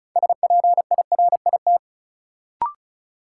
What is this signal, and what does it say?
Morse code, heard as a keyed tone, about 34 wpm, 700 Hz. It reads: SPIRIT
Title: Sentences from Top 1400 Words - New Word Drill Alt - 35wpm